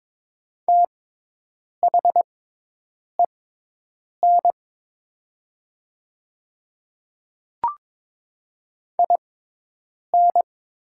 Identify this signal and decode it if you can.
Morse code, heard as a keyed tone, about 22 wpm, spaced as for 8 wpm, 700 Hz. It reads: THEN IN
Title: 6x Character Spacing - Top 100 Words - 22wpm